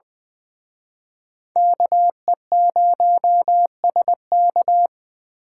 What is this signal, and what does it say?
Morse code, heard as a keyed tone, about 23 wpm, 700 Hz. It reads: KE0SK
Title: Speed Racing - USA 2x2 Call Signs 20wpm